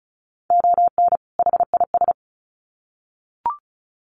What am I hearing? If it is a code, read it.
Morse code, heard as a keyed tone, about 34 wpm, 700 Hz. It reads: ON HIS
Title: Sets of 2 Words 35wpm